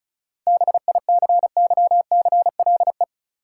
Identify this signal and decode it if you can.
Morse code, heard as a keyed tone, about 36 wpm, 700 Hz. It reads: BICYCLE
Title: Seven Letter Words - 35wpm